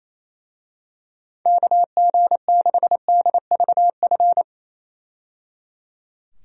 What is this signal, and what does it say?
Morse code, heard as a keyed tone, about 28 wpm, 700 Hz. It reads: KG6DVF
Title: USA Call Signs - 6 Characters - 28wpm